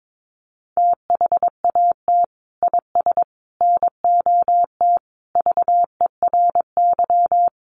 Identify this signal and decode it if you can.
Morse code, heard as a keyed tone, about 22 wpm, 700 Hz. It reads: THAT IS NOT VERY
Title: Sentences from Top 600 Words 22wpm